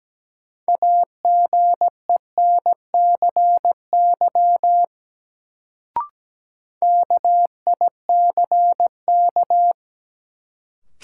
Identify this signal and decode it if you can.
Morse code, heard as a keyed tone, about 17 wpm, 700 Hz. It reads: AGENCY KICK
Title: Sentences from Top 1300 Words - New Word Drill Alt - 17wpm